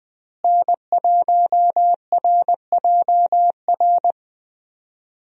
Morse code, heard as a keyed tone, about 20 wpm, 700 Hz. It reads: N1RJR
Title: USA Call Signs - 5 Characters - Repeated 3x - 20wpm